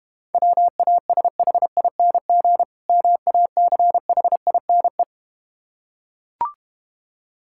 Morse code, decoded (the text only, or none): WASHING MACHINE